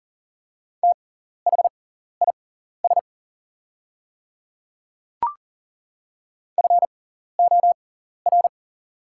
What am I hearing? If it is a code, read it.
Morse code, heard as a keyed tone, about 38 wpm, spaced as for 14 wpm, 700 Hz. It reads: THIS FOR
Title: 6x Character Spacing - Top 100 Words - 40wpm